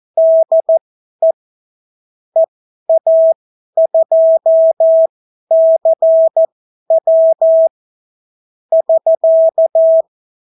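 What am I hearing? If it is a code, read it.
Morse code, heard as a keyed tone, about 14 wpm, spaced as for 11 wpm, 650 Hz. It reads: DE EA2CW <SK>
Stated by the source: Example Spanish CW QSO from a Morse learning book, 14 wpm characters with 8 wpm spacing.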